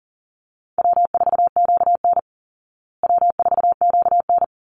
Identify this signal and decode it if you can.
Morse code, heard as a keyed tone, about 40 wpm, 700 Hz. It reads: W4QN W4QN
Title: Easy Call Signs Repeated 2x - 40wpm